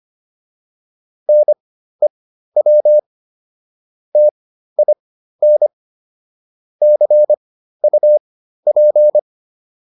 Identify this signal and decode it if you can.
Morse code, heard as a keyed tone, about 25 wpm, spaced as for 13 wpm, 600 Hz. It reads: NEW TIN CUP